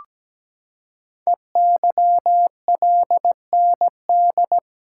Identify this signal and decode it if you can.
Morse code, heard as a keyed tone, about 17 wpm, 700 Hz. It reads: EYLND
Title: Code Groups - 5 Letters - 17wpm